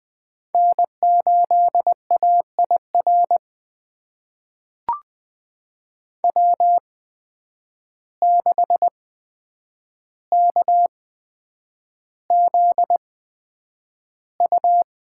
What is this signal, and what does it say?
Morse code, heard as a keyed tone, about 20 wpm, spaced as for 9 wpm, 700 Hz. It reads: N8AIR W6KZU
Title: USA Call Signs - 5 Characters - 8x Character Spacing - 20wpm